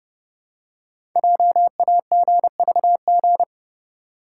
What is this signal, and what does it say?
Morse code, heard as a keyed tone, about 30 wpm, 700 Hz. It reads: JAGVG
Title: Code Groups - 5 Letters - 30wpm